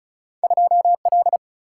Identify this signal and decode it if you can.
Morse code, heard as a keyed tone, about 34 wpm, 700 Hz. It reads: 2L
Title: Single-Letter Single-Number 35wpm